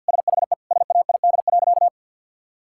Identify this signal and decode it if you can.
Morse code, heard as a keyed tone, about 53 wpm, 700 Hz. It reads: SHE SAID.
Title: Sentences from Top 1000 Words - Review Part 6 - 50wpm